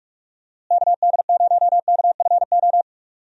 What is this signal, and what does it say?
Morse code, heard as a keyed tone, about 44 wpm, 700 Hz. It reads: KD0KFO
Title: USA Call Signs - 6 Characters - Repeated 3x - 45wpm